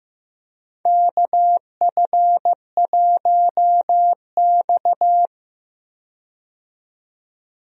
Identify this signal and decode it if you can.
Morse code, heard as a keyed tone, about 15 wpm, 700 Hz. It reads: KF1X
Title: USA Call Signs - 4 Characters - 15wpm